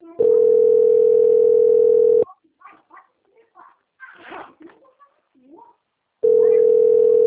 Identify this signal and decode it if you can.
Ringback tone (North America), 440 + 480 Hz, with speech.
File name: out- - - - - .50390